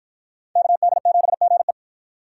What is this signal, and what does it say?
Morse code, heard as a keyed tone, about 53 wpm, 700 Hz. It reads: KD7GE